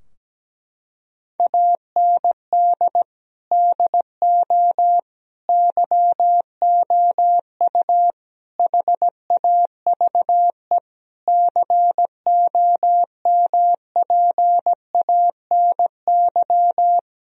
Morse code, heard as a keyed tone, about 17 wpm, 700 Hz. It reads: AND DO YOU HAVE COMPANY
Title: Sentences from Top 1000 Words - Review Part 3 - 17wpm